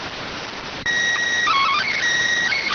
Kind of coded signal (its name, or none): SSTV (slow-scan television)